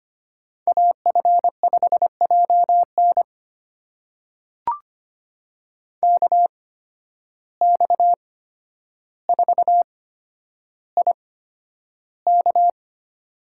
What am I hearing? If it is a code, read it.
Morse code, heard as a keyed tone, about 25 wpm, spaced as for 11 wpm, 700 Hz. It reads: AF5JN KX4IK
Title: USA Call Signs - 5 Characters - 8x Character Spacing - 25wpm